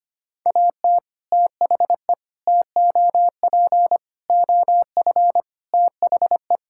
Morse code, heard as a keyed tone, about 25 wpm, 700 Hz. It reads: AT THE TOP OF THE
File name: practice / 25wpm / Sentences with 3-Word Phrases - 25wpm